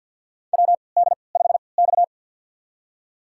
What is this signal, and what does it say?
Morse code, heard as a keyed tone, about 50 wpm, spaced as for 28 wpm, 700 Hz. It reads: WD5X
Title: USA Call Signs - 4 Characters - 3x Character Spacing - 50wpm